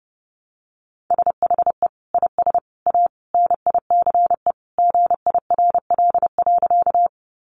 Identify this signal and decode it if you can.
Morse code, heard as a keyed tone, about 30 wpm, 700 Hz. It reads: SHE IS A NICE GIRL.
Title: Sentences from Top 1100 Words 30wpm